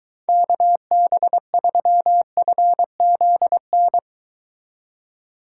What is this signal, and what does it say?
Morse code, heard as a keyed tone, about 23 wpm, 700 Hz. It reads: KB3FZN